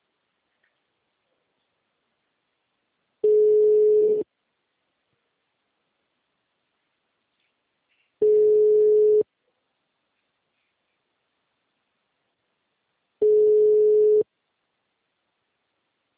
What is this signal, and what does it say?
Ringback tone (Europe, 425 Hz).